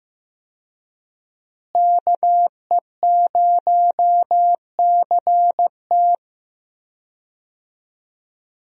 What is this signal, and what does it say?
Morse code, heard as a keyed tone, about 15 wpm, 700 Hz. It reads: KE0CT
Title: USA Call Signs - 5 Characters - Repeated 3x - 15wpm